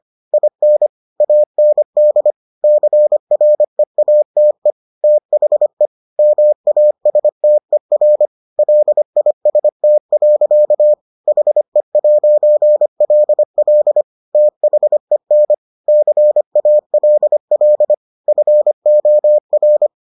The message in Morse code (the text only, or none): IN AND CREATE THE MASTER LIST. HE'LL THEN CALL FOR